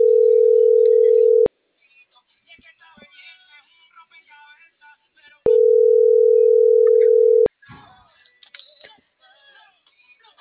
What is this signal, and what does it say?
Ringback tone (North America), 440 + 480 Hz.